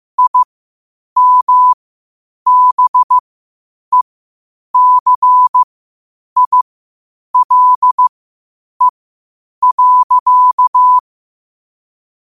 Morse code, heard as a keyed tone, about 15 wpm, spaced as for 8 wpm, 1000 Hz. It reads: IMBECILE.
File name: qotdfiles / 8wpm